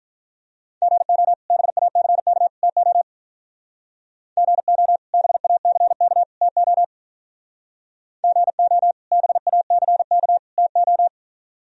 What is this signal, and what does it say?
Morse code, heard as a keyed tone, about 44 wpm, 700 Hz. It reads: GO BACK TO GO BACK TO GO BACK TO